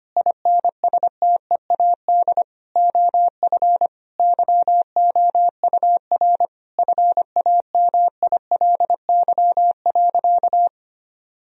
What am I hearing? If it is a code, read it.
Morse code, heard as a keyed tone, about 25 wpm, 700 Hz. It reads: INSTEAD OF YOUR FAMILY.
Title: Sentences from Top 1000 Words - Review Part 4 - 25wpm